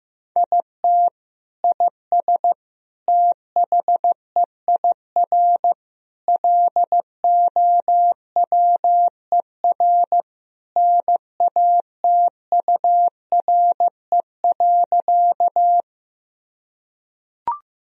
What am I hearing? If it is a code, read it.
Morse code, heard as a keyed tone, about 15 wpm, 700 Hz. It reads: IT IS THEIR LOWER NATURE.
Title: Sentences from Top 1000 Words - Review Part 4 - 15wpm